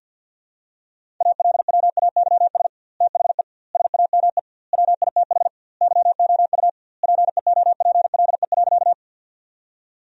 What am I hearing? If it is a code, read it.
Morse code, heard as a keyed tone, about 50 wpm, 700 Hz. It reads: ALWAYS THE SAME WITH YOU PEOPLE.
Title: 2x Word-Spacing - Sentences from Top 300 Words - 50wpm